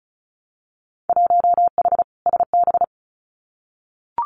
Morse code, heard as a keyed tone, about 36 wpm, 700 Hz. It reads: 1H SB
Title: Winter Field Day Exchanges 35wpm